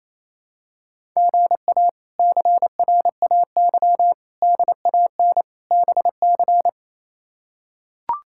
Morse code, heard as a keyed tone, about 28 wpm, 700 Hz. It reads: GA CRAY DAN BC
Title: K1USN SST QSO Practice - 28wpm